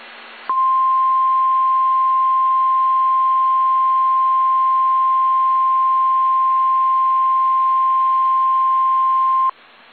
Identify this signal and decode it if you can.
Emergency Alert System (SAME) tones (the 1050 Hz attention tone). It is heard from 0.5 to 9.1 s.